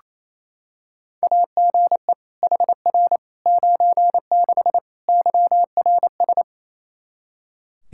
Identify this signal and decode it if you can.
Morse code, heard as a keyed tone, about 28 wpm, 700 Hz. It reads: AGE HR 96 YRS
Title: K1USN SST QSO Practice 28wpm